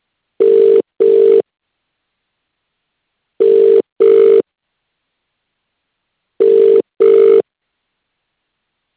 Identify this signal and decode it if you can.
Ringback tone (UK), 400 + 450 Hz.